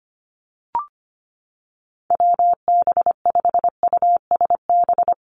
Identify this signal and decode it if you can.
Morse code, heard as a keyed tone, about 25 wpm, 700 Hz. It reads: WB5USB
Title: NA SOTA Activator Callsigns 25wpm